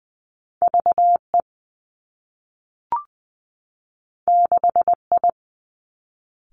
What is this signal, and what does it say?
Morse code, heard as a keyed tone, about 19 wpm, 700 Hz. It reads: V E 6 I
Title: Two Challenging Letter-Number Combinations 20wpm